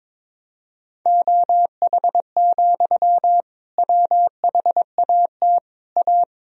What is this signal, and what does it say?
Morse code, heard as a keyed tone, about 22 wpm, 700 Hz. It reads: OH, WHAT A
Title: Sentences from Top 1000 Words - Review Part 2 - 22wpm